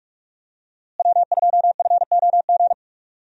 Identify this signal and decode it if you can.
Morse code, heard as a keyed tone, about 46 wpm, 700 Hz. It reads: W2FOG
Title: USA Call Signs - 5 Characters - Repeated 3x - 45wpm